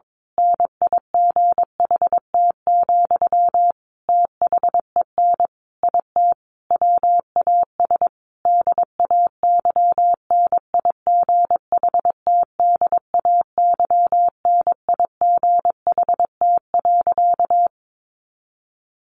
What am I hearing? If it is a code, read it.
Morse code, heard as a keyed tone, about 22 wpm, 700 Hz. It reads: NIGHT, THEN IT WAS DAY NIGHT DAY NIGHT.
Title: A Sound Of Thunder 22wpm